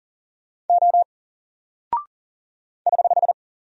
Morse code, heard as a keyed tone, about 40 wpm, 700 Hz. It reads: O <HH>
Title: Single Character (Letter:Number:Punctuation:Prosign) Alt 40wpm